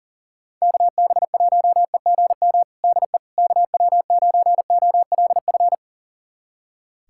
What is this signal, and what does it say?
Morse code, heard as a keyed tone, about 39 wpm, 700 Hz. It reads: KB1EGM DE KW9OLF
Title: USA Callsign DE Callsign 40wpm